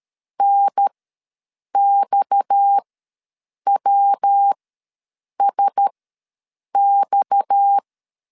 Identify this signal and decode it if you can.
Morse code, heard as a keyed tone, about 13 wpm, spaced as for 7 wpm, 800 Hz. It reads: NXWSX
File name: LEZIONI / 6 PKX0 / 6 PKX ws 12wpm 3106